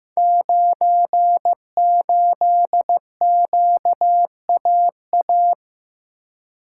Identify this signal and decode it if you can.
Morse code, heard as a keyed tone, about 15 wpm, 700 Hz. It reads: 98QAA
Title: Code Groups - 5 Letter:Numbers - 15wpm